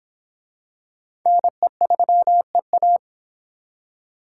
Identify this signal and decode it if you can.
Morse code, heard as a keyed tone, about 23 wpm, 700 Hz. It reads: NE3EA